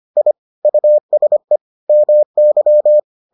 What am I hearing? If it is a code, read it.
Morse code, heard as a keyed tone, about 25 wpm, 600 Hz. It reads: I USE MY